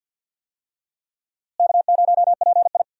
Morse code, heard as a keyed tone, about 53 wpm, 700 Hz. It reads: K0PI
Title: USA Call Signs - 4 Characters - Repeated 3x - 50wpm